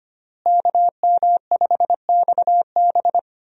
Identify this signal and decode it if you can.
Morse code, heard as a keyed tone, about 25 wpm, 700 Hz. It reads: KM5XB
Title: USA Call Signs - 5 Characters - Repeated 3x - 25wpm